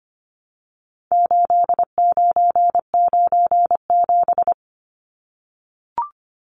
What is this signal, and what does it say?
Morse code, heard as a keyed tone, about 25 wpm, 700 Hz. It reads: 8997
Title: Numbers - 4-Digit 25wpm